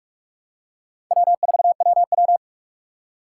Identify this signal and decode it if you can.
Morse code, heard as a keyed tone, about 46 wpm, 700 Hz. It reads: W4WW
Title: USA Call Signs - 3x Character Spacing - 45wpm